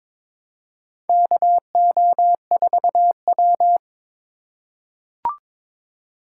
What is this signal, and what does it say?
Morse code, heard as a keyed tone, about 22 wpm, 700 Hz. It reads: KO4W
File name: practice / 22wpm / Four Letter-Number Combinations - 22wpm